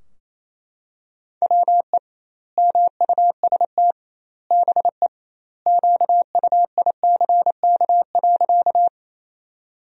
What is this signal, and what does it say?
Morse code, heard as a keyed tone, about 28 wpm, 700 Hz. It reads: WE MUST BE QUICK.
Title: 2x Word-Spacing - Sentences from Top 500 Words - 28wpm